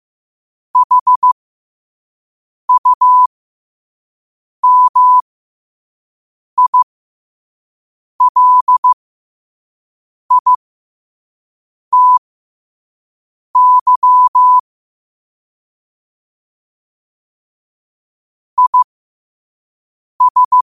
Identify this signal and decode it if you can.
Morse code, heard as a keyed tone, about 15 wpm, spaced as for 5 wpm, 1000 Hz. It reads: HUMILITY IS
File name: qotdfiles / 5wpm